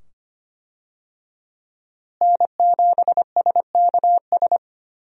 Morse code, heard as a keyed tone, about 25 wpm, 700 Hz. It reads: N7SKS
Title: USA Call Signs - 5 Characters - Repeated 2x - 25wpm